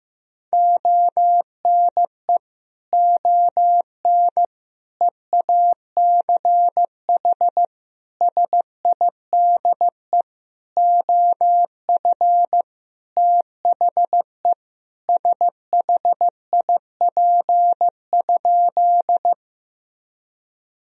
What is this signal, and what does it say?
Morse code, heard as a keyed tone, about 15 wpm, 700 Hz. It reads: ONE ON EACH SIDE OF THE SHIP?
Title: Sentences from Top 1000 Words - Review Part 2 - 15wpm